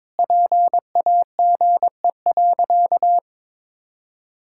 Morse code, heard as a keyed tone, about 22 wpm, 700 Hz. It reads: PAGE.
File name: practice / 22wpm / Sentences with 4-Word Phrases - 22wpm